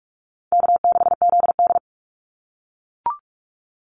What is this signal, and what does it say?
Morse code, heard as a keyed tone, about 44 wpm, 700 Hz. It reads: K6ZD
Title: Call Signs - 45wpm